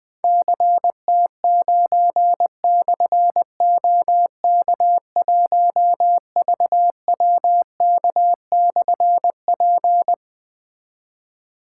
Morse code, heard as a keyed tone, about 20 wpm, 700 Hz. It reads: CT9/OK1VWK/P
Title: Difficult Call Signs 20wpm